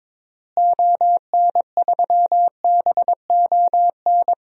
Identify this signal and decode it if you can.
Morse code, heard as a keyed tone, about 22 wpm, 700 Hz. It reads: ON3BON
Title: DXCC-100 Call Signs - 6 Characters - Repeated 2x - 22wpm